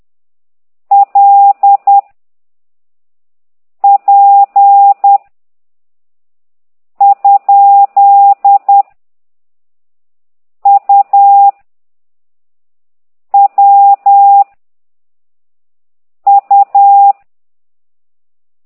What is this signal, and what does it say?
Morse code, heard as a keyed tone, about 10 wpm, 800 Hz. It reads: L P ? U W U
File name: ct2idw / exerc9